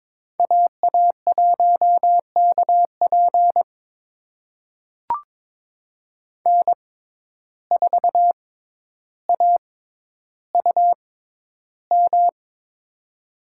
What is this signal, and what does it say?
Morse code, heard as a keyed tone, about 22 wpm, spaced as for 11 wpm, 700 Hz. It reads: AA1KP N4AUM
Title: USA Call Signs - 5 Characters - 6x Character Spacing - 22wpm